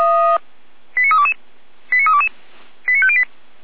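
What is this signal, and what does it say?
Five-tone selective call (ZVEI), heard as quick runs of five tones, tones of about 70 ms. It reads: 89322 89322 89508
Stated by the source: Five-tone ZVEI selective call received by SDR.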